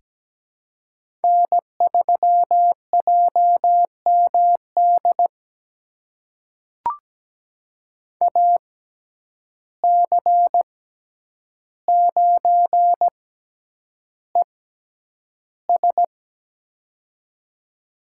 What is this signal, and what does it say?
Morse code, heard as a keyed tone, about 17 wpm, spaced as for 9 wpm, 700 Hz. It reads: N3JMD AC9ES
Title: USA Call Signs - 5 Characters - 6x Character Spacing - 17wpm